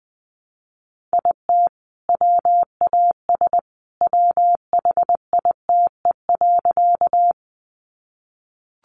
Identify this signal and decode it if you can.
Morse code, heard as a keyed tone, about 20 wpm, 700 Hz. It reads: IT WAS WHITE.
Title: Sentences from Top 300 Words 20wpm